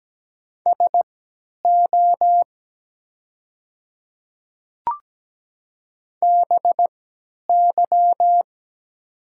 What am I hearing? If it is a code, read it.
Morse code, heard as a keyed tone, about 17 wpm, spaced as for 10 wpm, 700 Hz. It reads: SO BY
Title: 3x Character Spacing - Top 200 Words - 17wpm